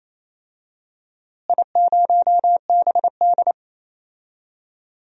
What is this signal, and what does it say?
Morse code, heard as a keyed tone, about 28 wpm, 700 Hz. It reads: I0BD
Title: DXCC-100 Call Signs - 4 Characters - Repeated 2x - 28wpm